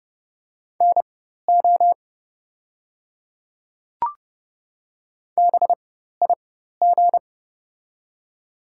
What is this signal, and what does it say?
Morse code, heard as a keyed tone, about 30 wpm, spaced as for 14 wpm, 700 Hz. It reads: NO BIG
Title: 4x Character Spacing - Top 200 Words - 30wpm